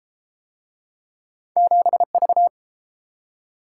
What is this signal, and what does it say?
Morse code, heard as a keyed tone, about 32 wpm, 700 Hz. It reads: 7V